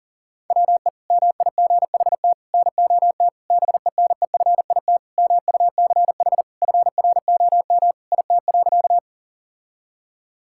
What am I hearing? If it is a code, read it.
Morse code, heard as a keyed tone, about 40 wpm, 700 Hz. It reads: WE MIGHT NOT BENEFIT MUCH FROM IT.
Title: Sentences from Top 1000 Words - Review Part 6 - 40wpm